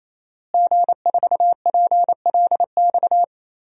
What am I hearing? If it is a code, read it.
Morse code, heard as a keyed tone, about 27 wpm, 700 Hz. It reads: G4PLX
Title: DXCC-100 Call Signs - 5 Characters - Repeated 3x - 28wpm